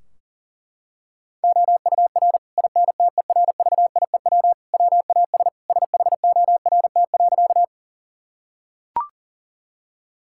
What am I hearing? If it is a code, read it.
Morse code, heard as a keyed tone, about 40 wpm, 700 Hz. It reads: OUR INTERVIEW WAS SHORT.
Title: Sentences from Top 1600 Words 40wpm